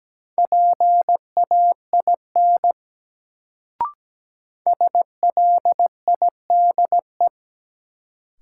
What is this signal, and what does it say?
Morse code, heard as a keyed tone, about 17 wpm, 700 Hz. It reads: PAIN SLIDE
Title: Sentences from Top 1600 Words - New Word Drill Alt - 17wpm